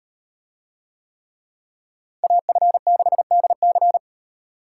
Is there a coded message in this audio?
Yes: Morse code from 2.2 to 4.0 s, heard as a keyed tone, about 38 wpm, 700 Hz. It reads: AF6DC